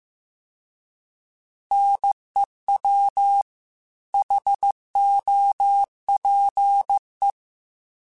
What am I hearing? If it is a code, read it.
Morse code, heard as a keyed tone, about 15 wpm, 800 Hz. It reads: NEW HOPE